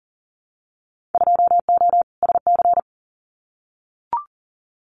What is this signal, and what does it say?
Morse code, heard as a keyed tone, about 40 wpm, 700 Hz. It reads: 2O SC
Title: Winter Field Day Exchanges 40wpm